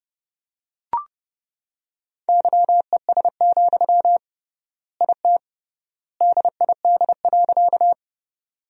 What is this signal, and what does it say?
Morse code, heard as a keyed tone, about 30 wpm, 700 Hz. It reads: YES, IT DID.
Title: 3x Word-Spacing - Sentences from Top 500 Words - 30wpm